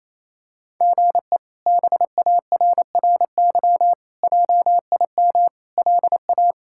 Morse code, heard as a keyed tone, about 27 wpm, 700 Hz. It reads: GE BARRY JIM LA
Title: K1USN SST QSO Practice - 28wpm